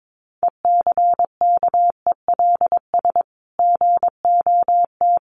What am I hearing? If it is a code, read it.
Morse code, heard as a keyed tone, about 22 wpm, 700 Hz. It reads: ECKELS GOT
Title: A Sound Of Thunder 22wpm